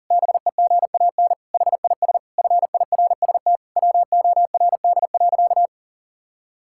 Morse code, heard as a keyed tone, about 37 wpm, 700 Hz. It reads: BEGAN HIS FIRST WORD.
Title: Pangram Challenge v2 - 40wpm